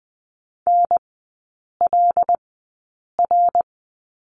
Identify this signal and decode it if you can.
Morse code, heard as a keyed tone, about 20 wpm, spaced as for 8 wpm, 700 Hz. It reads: NLR
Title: 093 Lesson 16e - Letter c - Three Characters 20wpm